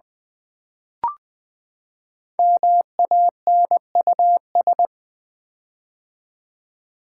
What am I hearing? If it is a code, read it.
Morse code, heard as a keyed tone, about 20 wpm, 700 Hz. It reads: MANUS
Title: Build A Word - 5 Letters At A Time - 10-Letter Words - 20wpm